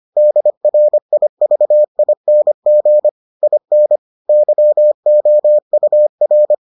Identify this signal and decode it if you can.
Morse code, heard as a keyed tone, about 25 wpm, 600 Hz. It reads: DRIVING IN YOUR